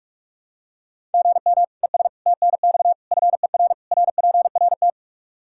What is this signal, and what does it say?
Morse code, heard as a keyed tone, about 44 wpm, 700 Hz. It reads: GM ES TNX FER RPRT
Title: K1USN SST QSO Practice 45wpm